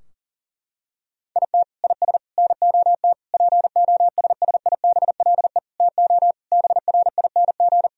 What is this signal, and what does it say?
Morse code, heard as a keyed tone, about 40 wpm, 700 Hz. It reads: IT IS NOT POSSIBLE TO BRING
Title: Sentences from Top 1000 Words - Review Part 4 - 40wpm